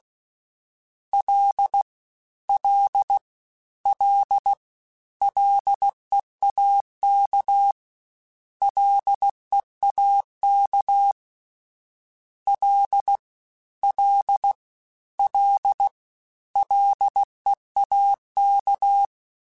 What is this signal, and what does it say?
Morse code, heard as a keyed tone, about 16 wpm, 800 Hz. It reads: L L L LEAK LEAK L L L LEAK
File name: Combinations / Combinations-mp3s / Combinations@15